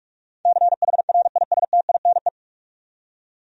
Morse code, heard as a keyed tone, about 46 wpm, 700 Hz. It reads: CHRISTINE
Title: USA Common Names - 45wpm